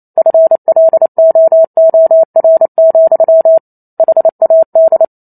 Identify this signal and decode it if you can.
Morse code, heard as a keyed tone, about 28 wpm, 650 Hz. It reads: FLOOR, HAD
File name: Chapter 2 - 28 WPM